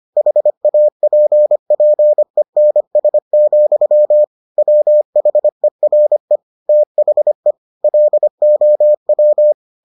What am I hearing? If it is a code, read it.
Morse code, heard as a keyed tone, about 25 wpm, 600 Hz. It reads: HAPPENS, WHERE THE LOW